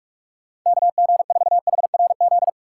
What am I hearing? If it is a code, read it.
Morse code, heard as a keyed tone, about 46 wpm, 700 Hz. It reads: KG4HRZ
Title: Speed Racing - USA Call Signs 45wpm